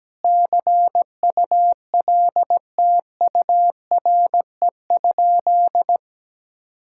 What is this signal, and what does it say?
Morse code, heard as a keyed tone, about 17 wpm, 700 Hz. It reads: CULTURE?
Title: Sentences from Top 1000 Words - Review Part 6 - 17wpm